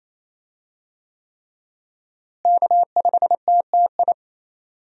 Morse code, heard as a keyed tone, about 28 wpm, 700 Hz. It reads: K5TTI
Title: USA Call Signs - 5 Characters - Repeated 3x - 28wpm